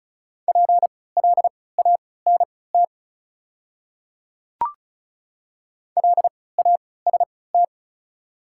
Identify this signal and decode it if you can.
Morse code, heard as a keyed tone, about 36 wpm, spaced as for 20 wpm, 690 Hz. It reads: PLANT LAST
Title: 3x Character Spacing - Top 500 Words - 35wpm